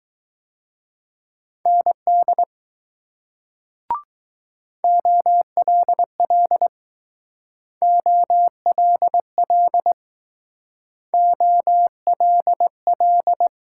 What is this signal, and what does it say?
Morse code, heard as a keyed tone, about 20 wpm, 700 Hz. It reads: ND OLL OLL OLL